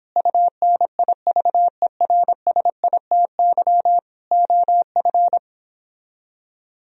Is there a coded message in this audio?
Yes: Morse code from 0.2 to 5.4 s, heard as a keyed tone, about 29 wpm, 700 Hz. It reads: UNIVERSITY OF